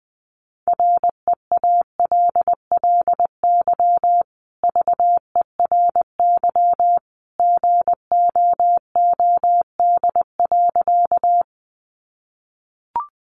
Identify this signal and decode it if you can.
Morse code, heard as a keyed tone, about 20 wpm, 700 Hz. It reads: REALLY VERY GOOD.
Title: Sentences from Top 600 Words 20wpm